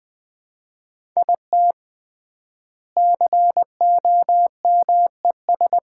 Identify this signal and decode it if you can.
Morse code, heard as a keyed tone, about 20 wpm, 700 Hz. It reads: IT COMES
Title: 3x Word-Spacing - Sets of 3 Words - 20wpm